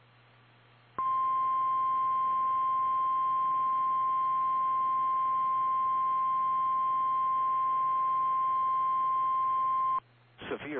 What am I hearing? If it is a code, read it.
Emergency Alert System (SAME) tones (the 1050 Hz attention tone), with speech. It is heard from 0.9 to 10.0 s.
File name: eas / KKPRKXPN / KKPRKXPN